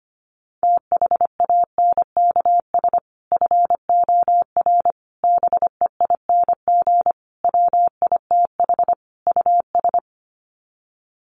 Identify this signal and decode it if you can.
Morse code, heard as a keyed tone, about 25 wpm, 700 Hz. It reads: THANKS FOR BEING WITH US
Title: Sets of 5 Words 25wpm